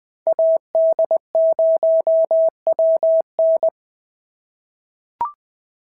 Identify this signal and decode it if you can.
Morse code, heard as a keyed tone, about 20 wpm, 650 Hz. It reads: AD0WN
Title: POTA QSO Practice - 20wpm